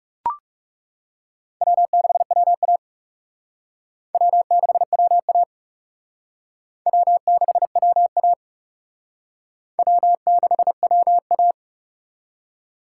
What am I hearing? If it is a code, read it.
Morse code, heard as a keyed tone, about 36 wpm, 700 Hz. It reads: W6WA W6WA W6WA W6WA
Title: Speed Racing - USA 1x2 Call Signs 30wpm